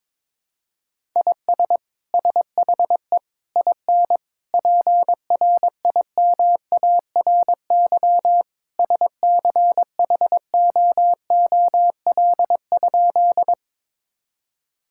Morse code, heard as a keyed tone, about 22 wpm, 700 Hz. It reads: IS SHE IN PRIMARY SCHOOL?
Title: Sentences from Top 1000 Words - Review Part 5 - 22wpm